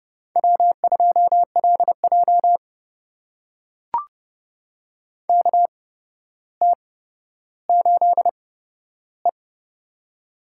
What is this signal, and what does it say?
Morse code, heard as a keyed tone, about 31 wpm, spaced as for 13 wpm, 700 Hz. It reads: W2LJ KT8E
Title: USA Call Signs - 4 Characters - 8x Character Spacing - 30wpm